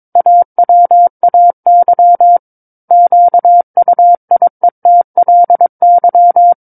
Morse code, heard as a keyed tone, about 22 wpm, 700 Hz. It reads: AWAY QUIETLY
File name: 06-Chapter-06 22 WPM